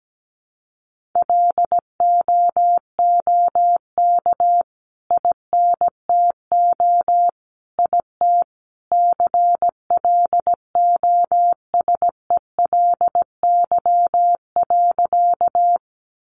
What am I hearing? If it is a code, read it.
Morse code, heard as a keyed tone, about 17 wpm, 700 Hz. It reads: LOOK INTO IT CLOSELY.
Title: Sentences from Top 1400 Words 17wpm